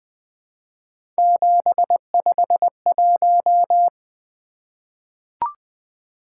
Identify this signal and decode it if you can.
Morse code, heard as a keyed tone, about 20 wpm, 700 Hz. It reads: 751
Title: Three Letter-Number Combinations - 20wpm